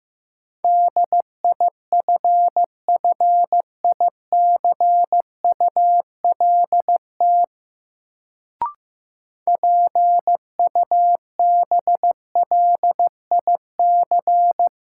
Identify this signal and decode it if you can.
Morse code, heard as a keyed tone, about 15 wpm, 700 Hz. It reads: DIFFICULT PUBLIC
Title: Sentences from Top 600 Words - New Word Drill - 15wpm